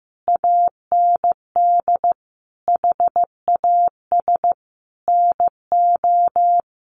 Morse code, heard as a keyed tone, about 15 wpm, 700 Hz. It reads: AND HAS NO
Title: Sentences from Top 1100 Words 15wpm